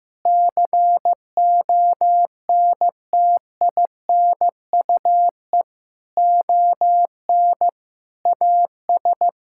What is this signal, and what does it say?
Morse code, heard as a keyed tone, about 15 wpm, 700 Hz. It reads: CONTINUE ON AS
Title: Sentences from Top 1000 Words - Review Part 6 - 15wpm